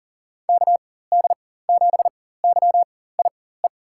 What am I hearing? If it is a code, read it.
Morse code, heard as a keyed tone, about 40 wpm, spaced as for 19 wpm, 700 Hz. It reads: KD7YIE
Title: USA Call Signs - 6 Characters - 4x Character Spacing - 40wpm